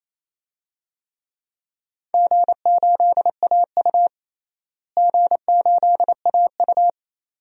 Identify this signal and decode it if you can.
Morse code, heard as a keyed tone, about 27 wpm, 700 Hz. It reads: G8AU G8AU
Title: DXCC-100 Call Signs - 4 Characters - Repeated 2x - 28wpm